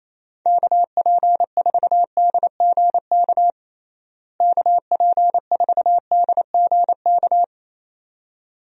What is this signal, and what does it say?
Morse code, heard as a keyed tone, about 28 wpm, 700 Hz. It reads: KP4DGK KP4DGK
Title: USA Call Signs - 6 Characters - Repeated 2x - 28wpm